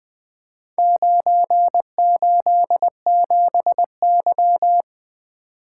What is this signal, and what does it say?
Morse code, heard as a keyed tone, about 20 wpm, 700 Hz. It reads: 987Y
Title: Four Letter-Number Combinations - 20wpm